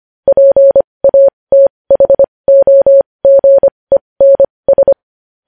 Morse code, heard as a keyed tone, about 25 wpm, spaced as for 20 wpm, 550 Hz. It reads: PATHOGENS
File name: podcast / News headlines-1x-25x20